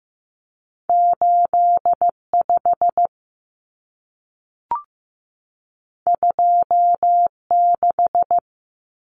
Morse code, heard as a keyed tone, about 15 wpm, 700 Hz. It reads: 85 26
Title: Numbers - 2-Digit 15wpm